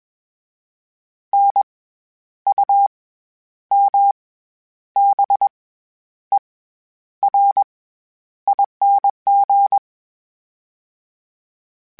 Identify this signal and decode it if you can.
Morse code, heard as a keyed tone, about 21 wpm, 800 Hz. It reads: N U M B E R ING
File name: Special INGs-mp3s / Special INGs-mp3s / SlowToFaster ing @20